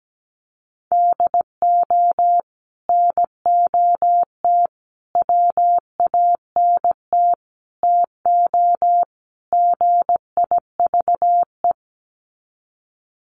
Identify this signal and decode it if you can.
Morse code, heard as a keyed tone, about 17 wpm, 700 Hz. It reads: DO NOT WANT TO GIVE
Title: Sets of 5 Words Encore 17wpm